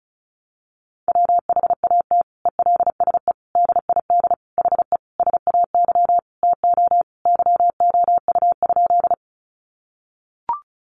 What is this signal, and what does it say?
Morse code, heard as a keyed tone, about 34 wpm, 700 Hz. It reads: WHAT ELSE DID HE SAY TO YOU?
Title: Sentences from Top 800 Words 35wpm